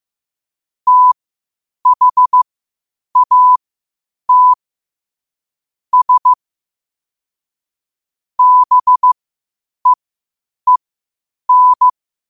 Morse code, heard as a keyed tone, about 15 wpm, spaced as for 8 wpm, 1000 Hz. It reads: THAT S BEEN
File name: qotdfiles / 8wpm